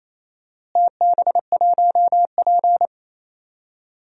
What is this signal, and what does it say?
Morse code, heard as a keyed tone, about 28 wpm, 700 Hz. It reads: TB1P
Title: DXCC-100 Call Signs - 4 Characters - Repeated 3x - 28wpm